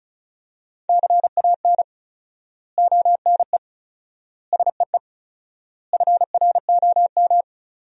Morse code, heard as a keyed tone, about 34 wpm, 700 Hz. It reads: CAN ONE SEE FROM
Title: 4x Word-Spacing - Sentences from Top 100 Words - 35wpm